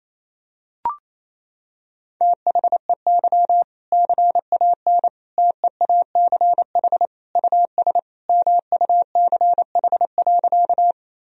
Morse code, heard as a keyed tone, about 28 wpm, 700 Hz. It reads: THEY CAN TEACH US MUCH.
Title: Sentences from Top 1000 Words - Review Part 2 - 28wpm